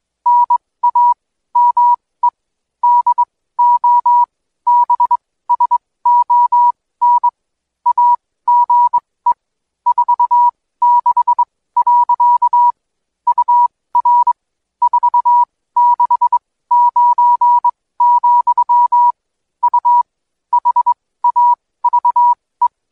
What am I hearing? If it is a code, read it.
Morse code, heard as a keyed tone, about 21 wpm, spaced as for 16 wpm, 970 Hz. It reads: NAME DOBSON AGE 46. UR 469, U HAVE
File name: Code Quick / AudioDownld2 / Seg125-DayDrill30-16WPM